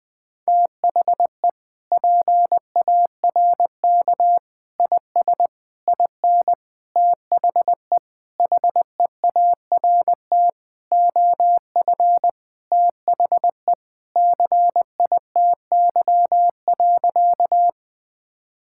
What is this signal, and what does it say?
Morse code, heard as a keyed tone, about 20 wpm, 700 Hz. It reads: THE PARK IS IN THE HEART OF THE CITY.
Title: Sentences with 4-Word Phrases - 20wpm